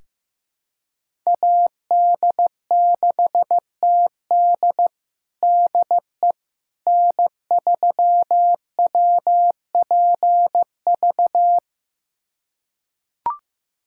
Morse code, heard as a keyed tone, about 15 wpm, 700 Hz. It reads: AD6TD DE N3WPV
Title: USA Callsign DE Callsign - 5 Characters - 15wpm